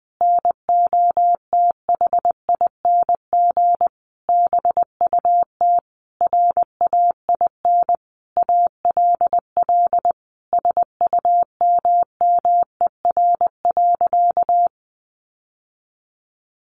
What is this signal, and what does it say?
Morse code, heard as a keyed tone, about 20 wpm, 700 Hz. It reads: NOTHING BUT RAIN ALL SUMMER.
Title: Sentences from Top 1100 Words 20wpm